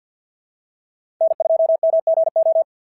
Morse code, heard as a keyed tone, about 47 wpm, 650 Hz. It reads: N2MGO
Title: SOTA QSO Practice 50wpm